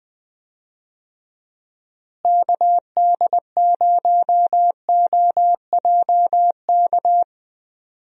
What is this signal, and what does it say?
Morse code, heard as a keyed tone, about 20 wpm, 700 Hz. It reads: KD0OJK
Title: USA Call Signs - 6 Characters - Repeated 3x - 20wpm